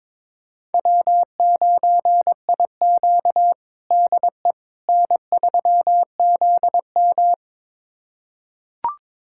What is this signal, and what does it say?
Morse code, heard as a keyed tone, about 22 wpm, 700 Hz. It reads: W9IQ DE N3ZM
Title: USA Callsign DE Callsign - 4 Characters - 22wpm